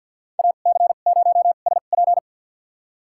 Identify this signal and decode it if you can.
Morse code, heard as a keyed tone, about 52 wpm, spaced as for 36 wpm, 700 Hz. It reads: AC0SP
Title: USA Call Signs - 5 Characters - 2x Character Spacing - 50wpm